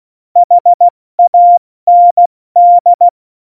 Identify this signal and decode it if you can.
Morse code, heard as a keyed tone, about 16 wpm, 700 Hz. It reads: HAND
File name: Morsekurs / Witze / Witz 16. Juli 2019, 70 F80